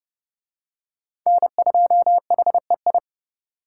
Morse code, heard as a keyed tone, about 31 wpm, 700 Hz. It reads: N2HEI